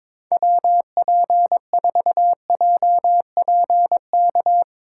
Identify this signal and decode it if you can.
Morse code, heard as a keyed tone, about 24 wpm, 700 Hz. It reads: WP4JPK
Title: Speed Racing - USA 2x3 Call Signs 22wpm